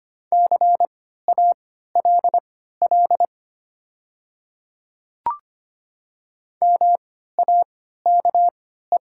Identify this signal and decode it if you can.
Morse code, heard as a keyed tone, about 25 wpm, spaced as for 14 wpm, 700 Hz. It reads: CALL MAKE
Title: 3x Character Spacing - Top 100 Words - 25wpm